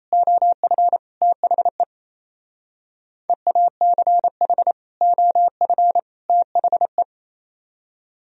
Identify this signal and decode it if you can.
Morse code, heard as a keyed tone, about 36 wpm, 700 Hz. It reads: OF THE EACH OF THE